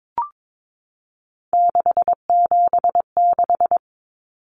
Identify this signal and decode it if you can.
Morse code, heard as a keyed tone, about 22 wpm, 700 Hz. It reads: 676
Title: Numbers - 3-Digit 22wpm